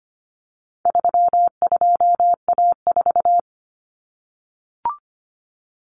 Morse code, heard as a keyed tone, about 25 wpm, 700 Hz. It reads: 32A4
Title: Common Cut Numbers - 4-Digit 25wpm